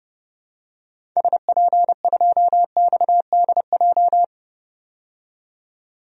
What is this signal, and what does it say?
Morse code, heard as a keyed tone, about 30 wpm, 700 Hz. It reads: SP2XDJ
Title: DXCC-100 Call Signs - 6 Characters - 30wpm